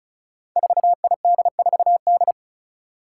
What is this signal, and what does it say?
Morse code, heard as a keyed tone, about 36 wpm, 700 Hz. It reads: 4ID4D
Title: Five Challenging Letter-Number Combinations 35wpm